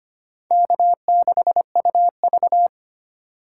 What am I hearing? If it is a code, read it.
Morse code, heard as a keyed tone, about 32 wpm, 700 Hz. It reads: K6UV